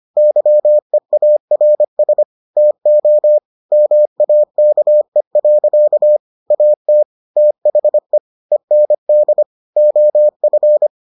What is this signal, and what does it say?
Morse code, heard as a keyed tone, about 25 wpm, 600 Hz. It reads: YEARS TO MAKE. AT THE END OF